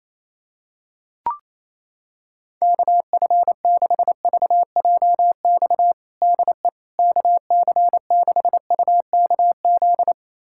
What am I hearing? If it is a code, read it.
Morse code, heard as a keyed tone, about 28 wpm, 700 Hz. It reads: KF6VJX DE KC6UKZ
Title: USA Callsign DE Callsign - 6 Characters - 28wpm